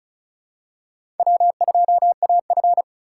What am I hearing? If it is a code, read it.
Morse code, heard as a keyed tone, about 34 wpm, 700 Hz. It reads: W2AF
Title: USA Call Signs - 4 Characters - 3x Character Spacing - 35wpm